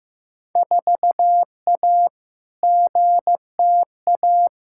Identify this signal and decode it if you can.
Morse code, heard as a keyed tone, about 15 wpm, 700 Hz. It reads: 4A GTA
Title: Field Day Exchanges Encore - 15wpm